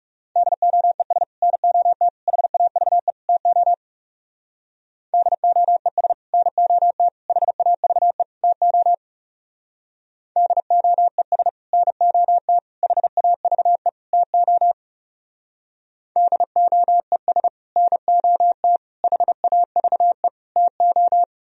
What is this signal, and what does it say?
Morse code, heard as a keyed tone, about 37 wpm, 700 Hz. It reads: DOES NOT HAVE TO DOES NOT HAVE TO DOES NOT HAVE TO DOES NOT HAVE TO
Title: Speed-Racing - Sets of 4 Words 30wpm